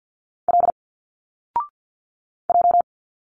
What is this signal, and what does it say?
Morse code, heard as a keyed tone, about 49 wpm, 700 Hz. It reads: L J
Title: Single Letters 50wpm